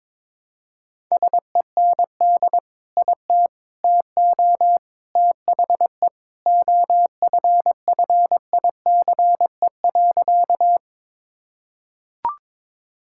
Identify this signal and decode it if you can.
Morse code, heard as a keyed tone, about 21 wpm, 700 Hz. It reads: SEND IT TO THE OFFICE.
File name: practice / 22wpm / Sentences with 3-Word Phrases - 22wpm